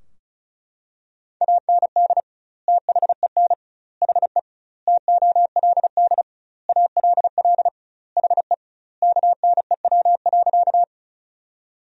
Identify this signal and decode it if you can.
Morse code, heard as a keyed tone, about 34 wpm, 700 Hz. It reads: AND THEN HE TOLD ALL HE KNEW.
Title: 2x Word-Spacing - Sentences from Top 400 Words - 35wpm